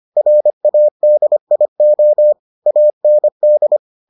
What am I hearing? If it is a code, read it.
Morse code, heard as a keyed tone, about 25 wpm, 600 Hz. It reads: RADIO AND